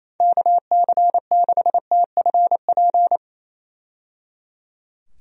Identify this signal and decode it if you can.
Morse code, heard as a keyed tone, about 27 wpm, 700 Hz. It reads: KC6TFP
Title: USA Call Signs - 6 Characters - 28wpm